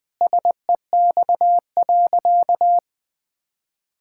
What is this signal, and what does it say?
Morse code, heard as a keyed tone, about 20 wpm, 700 Hz. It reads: SEX.
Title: Sentences from Top 1600 Words 20wpm